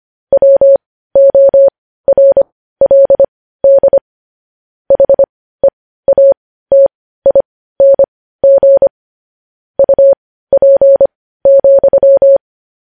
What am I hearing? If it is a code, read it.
Morse code, heard as a keyed tone, about 25 wpm, spaced as for 15 wpm, 550 Hz. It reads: WORLD HEATING UP,
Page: news headlines in morse code at 15 wpm